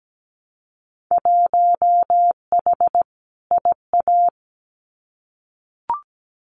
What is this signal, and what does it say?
Morse code, heard as a keyed tone, about 17 wpm, 700 Hz. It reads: 1H IA
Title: Winter Field Day Exchanges 17wpm